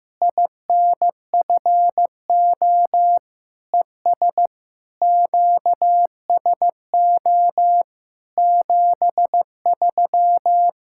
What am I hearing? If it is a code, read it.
Morse code, heard as a keyed tone, about 15 wpm, 700 Hz. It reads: INFO ES QSO 73
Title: Practice QSO - Exchange 4 - Ending - 15wpm